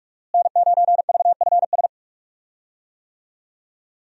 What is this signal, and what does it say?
Morse code, heard as a keyed tone, about 42 wpm, 700 Hz. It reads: N9VFS
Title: USA Call Signs - 5 Characters - Repeated 2x - 45wpm